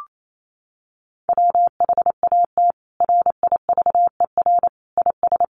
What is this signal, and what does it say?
Morse code, heard as a keyed tone, about 28 wpm, 700 Hz. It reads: WHAT RIVER IS
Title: Sentences from Top 300 Words 28wpm